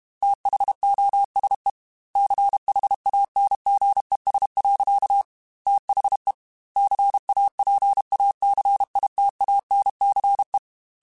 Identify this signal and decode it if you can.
Morse code, heard as a keyed tone, about 32 wpm, 800 Hz. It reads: THOSE CHANGES. THE CAPACITANCE
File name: Technical Subjects / Technical Subjects-mp3s / XC @30